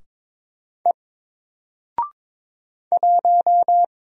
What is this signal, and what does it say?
Morse code, heard as a keyed tone, about 22 wpm, 700 Hz. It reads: E 1
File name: Single-Letter-Number-22wpm